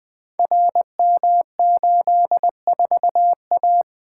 Morse code, heard as a keyed tone, about 20 wpm, 700 Hz. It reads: RM84A
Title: Code Groups - 5 Letter:Numbers - 20wpm